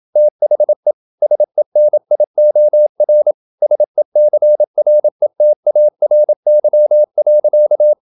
Morse code, heard as a keyed tone, about 27 wpm, 600 Hz. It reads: THE SENIOR SECRETARY.
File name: Never 0027wpm